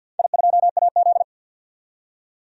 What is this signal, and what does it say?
Morse code, heard as a keyed tone, about 51 wpm, 700 Hz. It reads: I2AZ
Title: DXCC-100 Call Signs - 4 Characters - 50wpm